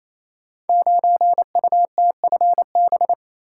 Morse code, heard as a keyed tone, about 28 wpm, 700 Hz. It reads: 9UTFB
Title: Code Groups - 5 Letter:Numbers - 28wpm